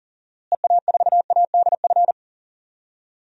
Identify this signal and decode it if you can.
Morse code, heard as a keyed tone, about 40 wpm, 700 Hz. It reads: EA4ADF
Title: DXCC-100 Call Signs - 6 Characters - Repeated 2x - 40wpm